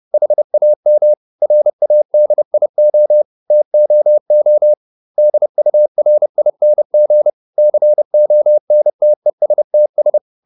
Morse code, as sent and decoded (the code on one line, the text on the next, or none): .... .- -- / .-. .- -.. .. --- / - --- --- / -.. ..- .-. .. -. --. / -.-. --- -. - . ... - ...
HAM RADIO TOO DURING CONTESTS